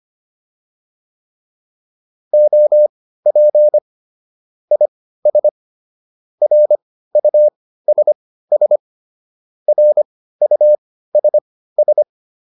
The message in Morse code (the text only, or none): OP IS RUSS RUSS